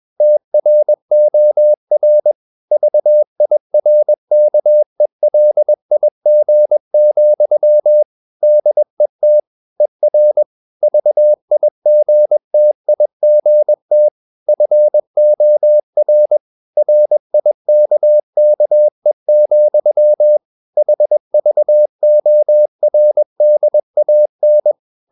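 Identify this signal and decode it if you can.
Morse code, heard as a keyed tone, about 21 wpm, 600 Hz. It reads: TROR VIRKELIG, DET ER VIGTIGT FOR RIKKE, HVORDAN
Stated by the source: Morse practice at 21 wpm, slow-speed holiday exercise.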